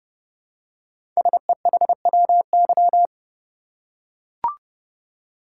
Morse code, heard as a keyed tone, about 30 wpm, 700 Hz. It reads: SEHWY
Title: Code Groups - 5 Letters - 30wpm